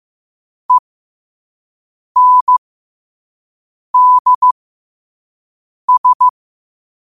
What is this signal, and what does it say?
Morse code, heard as a keyed tone, about 15 wpm, spaced as for 5 wpm, 1000 Hz. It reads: ENDS
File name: qotdfiles / 5wpm